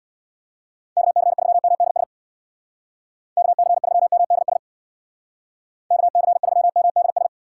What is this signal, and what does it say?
Morse code, heard as a keyed tone, about 62 wpm, 700 Hz. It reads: KC3MDS KC3MDS KC3MDS
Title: Speed Racing - USA 2x3 Call Signs 50wpm